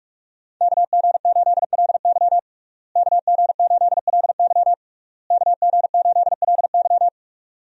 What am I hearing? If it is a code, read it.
Morse code, heard as a keyed tone, about 46 wpm, 700 Hz. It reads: KG8LY KG8LY KG8LY
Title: USA Call Signs - 5 Characters - Repeated 3x - 45wpm